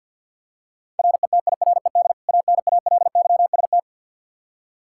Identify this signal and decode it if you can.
Morse code, heard as a keyed tone, about 52 wpm, 700 Hz. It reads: RETIRED ANALYST
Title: Practice QSO - Exchange 3 - More Info - 50wpm